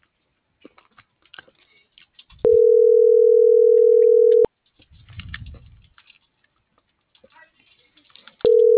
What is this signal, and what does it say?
Ringback tone (North America), 440 + 480 Hz.